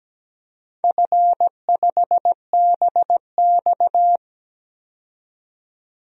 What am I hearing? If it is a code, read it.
Morse code, heard as a keyed tone, about 17 wpm, 700 Hz. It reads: F5BX
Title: DXCC-100 Call Signs - 4 Characters - 17wpm